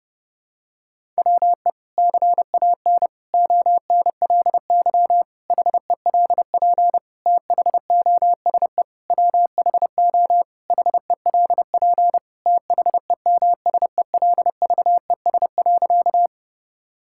Morse code, heard as a keyed tone, about 30 wpm, 700 Hz. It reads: WE CAN ONLY HELP THOSE WHO HELP THEMSELVES.
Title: Sentences from Top 1000 Words - Review Part 5 - 30wpm